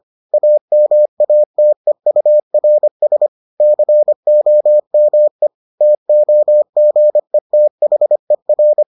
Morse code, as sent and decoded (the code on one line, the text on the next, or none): .- -- .- - . ..- .-. ... / -.-. --- -- . / - --- --. - .... .-.
AMATEURS COME TOGETHER